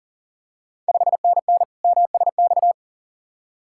Morse code, heard as a keyed tone, about 42 wpm, 690 Hz. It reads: 5NN MSX
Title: Ontario QSO Party Exchange 40wpm